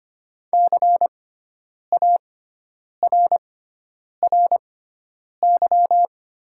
Morse code, heard as a keyed tone, about 25 wpm, spaced as for 9 wpm, 700 Hz. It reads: CARRY
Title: 6x Character Spacing - Top 300 Words - 25wpm